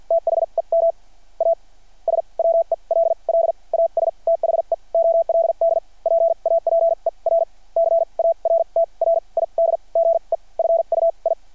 Morse code, heard as a keyed tone, about 51 wpm, spaced as for 36 wpm, 660 Hz. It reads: THEM AS WELL AS THE OLD PAPER CARTRIDGE FUI
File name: archived / CRSnet